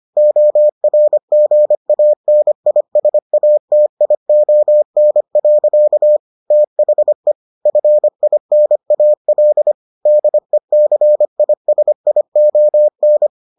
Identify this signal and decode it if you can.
Morse code, heard as a keyed tone, about 25 wpm, 600 Hz. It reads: ORGANISATION. THE FINAL DECISION